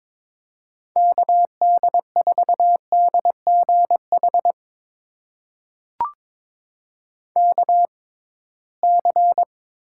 Morse code, heard as a keyed tone, about 22 wpm, spaced as for 17 wpm, 700 Hz. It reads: KD4DGH KC
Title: USA Call Signs - 6 Characters - 6x Character Spacing - 22wpm